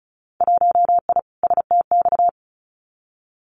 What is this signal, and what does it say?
Morse code, heard as a keyed tone, about 34 wpm, 700 Hz. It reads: 1I STX
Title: Winter Field Day Exchanges 35wpm